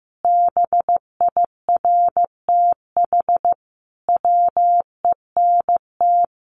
Morse code, heard as a keyed tone, about 15 wpm, 700 Hz. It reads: BIRTH WENT
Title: Sentences from Top 800 Words 15wpm